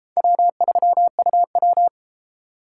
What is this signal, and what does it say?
Morse code, heard as a keyed tone, about 30 wpm, 700 Hz. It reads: W3UW